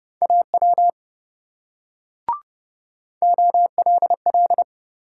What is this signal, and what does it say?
Morse code, heard as a keyed tone, about 26 wpm, 700 Hz. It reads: AW OLL